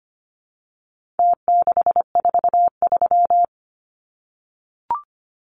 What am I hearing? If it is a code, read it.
Morse code, heard as a keyed tone, about 25 wpm, 700 Hz. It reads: T643
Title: Numbers - Contest Style - 4-Digit 25wpm